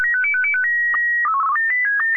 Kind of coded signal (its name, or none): SSTV (slow-scan television)